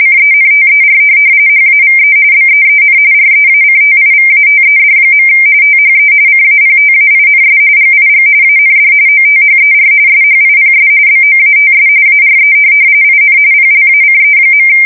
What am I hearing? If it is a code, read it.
RTTY, about 45.45 baud. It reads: CENTER HAVE BEEN REACTIVATED. THE CAPTAINS GIG AND OTHER SHIPS BOATS ARE IN PLACE.
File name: fsk170 / USS-New-Jersey